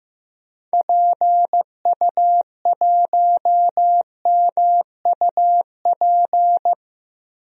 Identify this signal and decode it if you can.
Morse code, heard as a keyed tone, about 15 wpm, 700 Hz. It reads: PU1MUP
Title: DXCC-100 Call Signs - 6 Characters - Repeated 2x - 15wpm